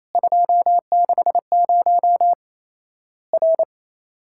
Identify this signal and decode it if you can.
Morse code, heard as a keyed tone, about 27 wpm, 700 Hz. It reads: 260 R
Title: Worked All Europe DX Contest - QTC Exchange - 28wpm